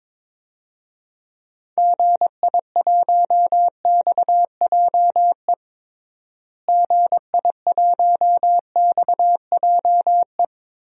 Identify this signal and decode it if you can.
Morse code, heard as a keyed tone, about 22 wpm, 700 Hz. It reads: GI1XJE GI1XJE
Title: DXCC-100 Call Signs - 6 Characters - Repeated 2x - 22wpm